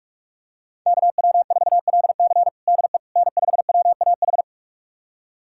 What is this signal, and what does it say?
Morse code, heard as a keyed tone, about 47 wpm, 700 Hz. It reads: KW4LC DE N5WAH
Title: USA Callsign DE Callsign - 5 Characters - 45wpm